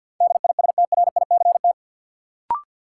Morse code, heard as a keyed tone, about 53 wpm, 700 Hz. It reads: DISTRICT
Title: Sentences from Top 1600 Words - New Word Drill Alt - 50wpm